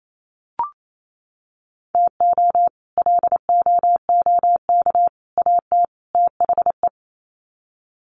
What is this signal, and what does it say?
Morse code, heard as a keyed tone, about 28 wpm, 700 Hz. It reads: TO LOOK AT THE
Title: Sets of 4 Words 28wpm